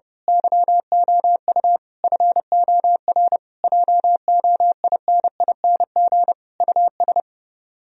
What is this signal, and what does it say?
Morse code, heard as a keyed tone, about 30 wpm, 700 Hz. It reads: YOU FOR JOINING US
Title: Speed-Racing - Sets of 5 Words 30wpm